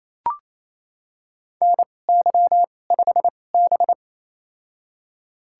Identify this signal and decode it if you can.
Morse code, heard as a keyed tone, about 28 wpm, spaced as for 20 wpm, 700 Hz. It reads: NY5B
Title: USA Call Signs - 4 Characters - 2x Character Spacing - 28wpm